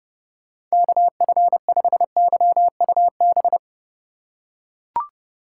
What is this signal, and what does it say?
Morse code, heard as a keyed tone, about 30 wpm, 700 Hz. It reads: KF5YUB
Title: K1USN SST QSO Practice - 30wpm